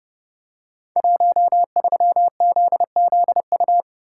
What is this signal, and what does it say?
Morse code, heard as a keyed tone, about 30 wpm, 700 Hz. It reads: 13ZZU
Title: Code Groups - 5 Letter:Numbers - 30wpm